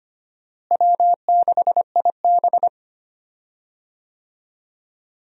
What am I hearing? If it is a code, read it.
Morse code, heard as a keyed tone, about 25 wpm, 700 Hz. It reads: W6IB
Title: USA Call Signs - 4 Characters - Repeated 3x - 25wpm